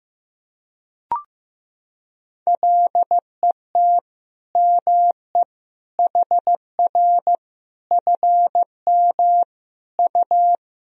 Morse code, heard as a keyed tone, about 15 wpm, 700 Hz. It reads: LET ME HR FM U
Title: Sentences from Top 400 Words and Abbreviations 15wpm